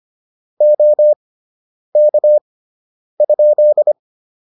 Morse code, heard as a keyed tone, about 25 wpm, spaced as for 9 wpm, 600 Hz. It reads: OK?